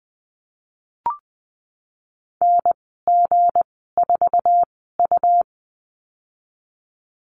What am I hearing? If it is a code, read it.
Morse code, heard as a keyed tone, about 19 wpm, spaced as for 14 wpm, 700 Hz. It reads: NG4U
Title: 186 Lesson 30g - Letter b - Call Signs 20wpm